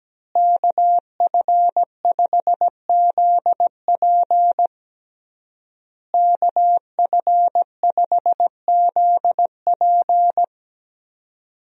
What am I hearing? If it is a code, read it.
Morse code, heard as a keyed tone, about 17 wpm, 700 Hz. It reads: KF5ZP KF5ZP
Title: USA Call Signs - 5 Characters - Repeated 3x - 17wpm